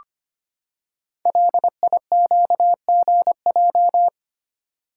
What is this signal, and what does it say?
Morse code, heard as a keyed tone, about 25 wpm, 700 Hz. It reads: LIQGJ
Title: Code Groups - 5 Letters - 25wpm